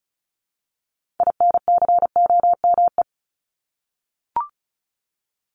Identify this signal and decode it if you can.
Morse code, heard as a keyed tone, about 34 wpm, 700 Hz. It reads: INCOME
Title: Top 751-1000 Words 35wpm